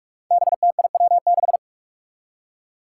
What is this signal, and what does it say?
Morse code, heard as a keyed tone, about 46 wpm, 700 Hz. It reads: BTIW6
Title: Five Challenging Letter-Number Combinations 45wpm